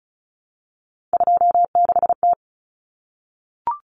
Morse code, heard as a keyed tone, about 34 wpm, 700 Hz. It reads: 26T
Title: Common Cut Numbers - 3-Digit 35wpm